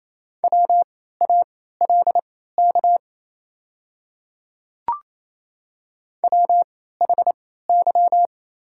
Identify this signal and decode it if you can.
Morse code, heard as a keyed tone, about 27 wpm, spaced as for 16 wpm, 700 Hz. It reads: WALK WHY
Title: 3x Character Spacing - Top 300 Words - 28wpm